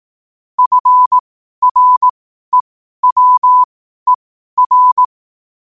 Morse code, heard as a keyed tone, about 18 wpm, spaced as for 12 wpm, 1000 Hz. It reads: FREWER
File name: qotdfiles / 12wpm